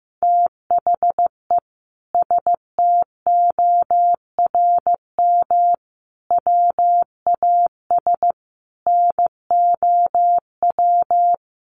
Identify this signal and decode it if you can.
Morse code, heard as a keyed tone, about 15 wpm, 700 Hz. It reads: THE STORM WAS NOW
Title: Sentences from Top 1500 Words 15wpm